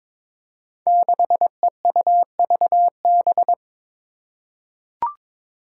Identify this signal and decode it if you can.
Morse code, heard as a keyed tone, about 22 wpm, 700 Hz. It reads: 6EUVB
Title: Five Challenging Letter-Number Combinations 22wpm